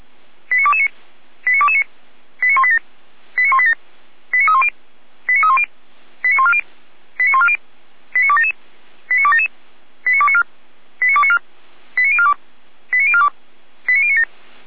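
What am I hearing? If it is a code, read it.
Five-tone selective call (ZVEI), heard as quick runs of five tones, tones of about 70 ms. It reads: 89229 89229 89187 89187 89310 89310 89140 89140 89277 89277 89294 89294 89042 89042 89097
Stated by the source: ZVEI five-tone call received by SDR.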